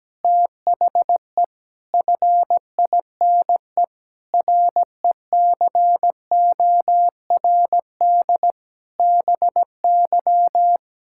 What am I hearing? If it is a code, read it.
Morse code, heard as a keyed tone, about 17 wpm, 700 Hz. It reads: THE FINE RECORD BY
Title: Pangram Challenge v2 - 17wpm